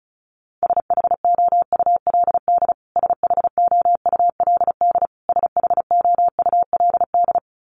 Morse code, heard as a keyed tone, about 34 wpm, 700 Hz. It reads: SHOULD SHOULD SHOULD
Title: Top 500 Words Repeated 35wpm